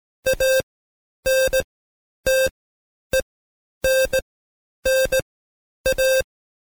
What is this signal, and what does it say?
Morse code, heard as a keyed tone, about 17 wpm, spaced as for 9 wpm, 530 Hz. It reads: ANTENNA
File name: Code Quick / AudioDownld2 / Seg131-Stepper4 9WPM